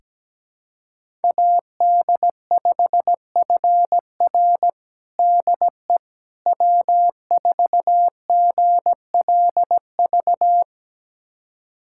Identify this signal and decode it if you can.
Morse code, heard as a keyed tone, about 17 wpm, 700 Hz. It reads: AD5FR DE W4GLV
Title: USA Callsign DE Callsign - 5 Characters - 17wpm